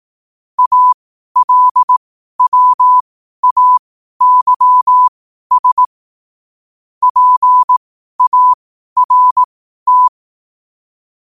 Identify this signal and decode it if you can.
Morse code, heard as a keyed tone, about 18 wpm, spaced as for 12 wpm, 1000 Hz. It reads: ALWAYS PART
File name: qotdfiles / 12wpm